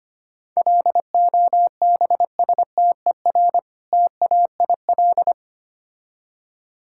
Morse code, heard as a keyed tone, about 25 wpm, 700 Hz. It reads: LOBSTER TAIL
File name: practice / 25wpm / Difficult Words with Hints - 25wpm